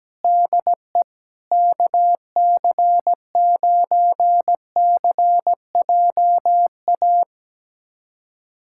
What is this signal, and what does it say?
Morse code, heard as a keyed tone, about 17 wpm, 700 Hz. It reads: DE KC9CJA
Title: USA Callsign DE Callsign - 6 Characters - 17wpm